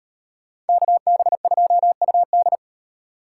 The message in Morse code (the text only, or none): KB2UD